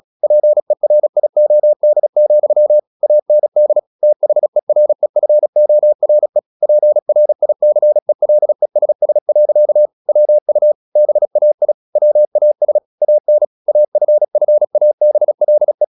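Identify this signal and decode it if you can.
Morse code, heard as a keyed tone, about 36 wpm, 600 Hz. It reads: PERIOD, AND THEREFORE PRICELESS. WU BAI WAS AN AFFABLE
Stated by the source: Morse practice at 36 wpm, text of Ken Follett's novel Never, from CW med Gnister.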